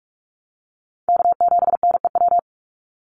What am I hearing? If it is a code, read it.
Morse code, heard as a keyed tone, about 46 wpm, 700 Hz. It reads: K7NEW
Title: NA SOTA Chaser Callsigns 45wpm